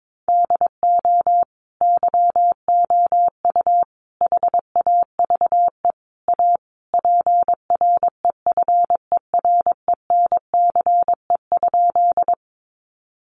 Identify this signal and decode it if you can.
Morse code, heard as a keyed tone, about 22 wpm, 700 Hz. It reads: DO YOU HAVE A PREFERENCE?
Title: Sentences from Top 1500 Words 22wpm